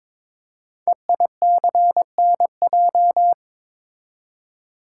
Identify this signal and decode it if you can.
Morse code, heard as a keyed tone, about 22 wpm, 700 Hz. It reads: EICNJ
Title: Code Groups - 5 Letters - 22wpm